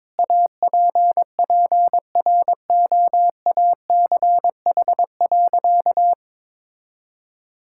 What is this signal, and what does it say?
Morse code, heard as a keyed tone, about 22 wpm, 700 Hz. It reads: APPROACH.
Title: Sentences from Top 1000 Words - Review Part 6 - 22wpm